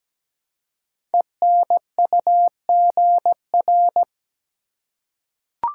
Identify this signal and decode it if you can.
Morse code, heard as a keyed tone, about 17 wpm, 700 Hz. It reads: ENUGR
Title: Code Groups - 5 Letters - 17wpm